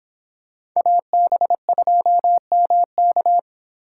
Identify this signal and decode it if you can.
Morse code, heard as a keyed tone, about 26 wpm, 700 Hz. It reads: AB2MK